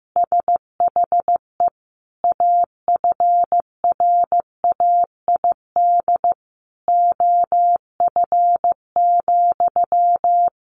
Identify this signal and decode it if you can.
Morse code, heard as a keyed tone, about 15 wpm, 700 Hz. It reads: SHE AFRAID OF,
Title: Sentences from Top 1500 Words 15wpm